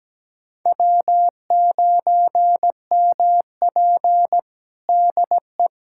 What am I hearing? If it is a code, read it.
Morse code, heard as a keyed tone, about 17 wpm, 700 Hz. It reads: W9MP DE
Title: USA Callsign DE Callsign - 4 Characters - 17wpm